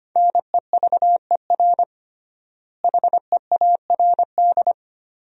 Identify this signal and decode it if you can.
Morse code, heard as a keyed tone, about 25 wpm, 700 Hz. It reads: NEVER HEARD
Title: 3x Word-Spacing - Sentences from Top 400 Words - 25wpm